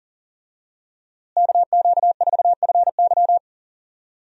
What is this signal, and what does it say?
Morse code, heard as a keyed tone, about 40 wpm, 700 Hz. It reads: KQ4FY
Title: USA Call Signs - 5 Characters - Repeated 2x - 40wpm